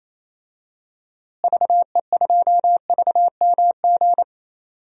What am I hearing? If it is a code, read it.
Morse code, heard as a keyed tone, about 28 wpm, 700 Hz. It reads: VE2VMG
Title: DXCC-100 Call Signs - 6 Characters - Repeated 2x - 28wpm